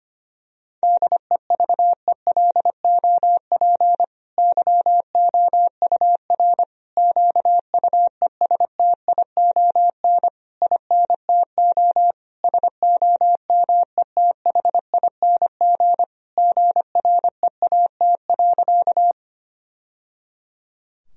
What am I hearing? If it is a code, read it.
Morse code, heard as a keyed tone, about 25 wpm, 700 Hz. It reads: DEVELOP YOUR QUESTION INTO SOMETHING GREAT.
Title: Sentences from Top 1000 Words - Review Part 2 - 25wpm